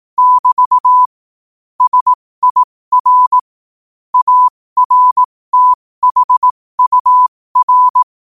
Morse code, heard as a keyed tone, about 18 wpm, spaced as for 15 wpm, 1000 Hz. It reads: <BT> SIR ARTHUR
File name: qotdfiles / 15wpm